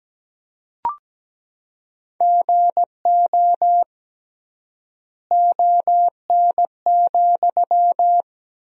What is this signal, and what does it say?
Morse code, heard as a keyed tone, about 17 wpm, 700 Hz. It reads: GO ON,
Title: 3x Word-Spacing - Sentences from Top 100 Words - 17wpm